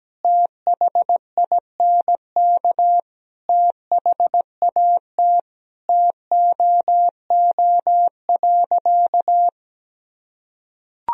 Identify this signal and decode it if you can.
Morse code, heard as a keyed tone, about 17 wpm, 700 Hz. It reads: THINK THAT TOO.
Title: Sentences from Top 1000 Words - Review Part 3 - 17wpm